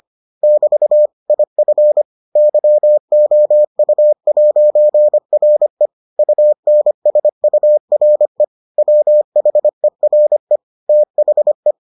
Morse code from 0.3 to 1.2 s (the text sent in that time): <BT>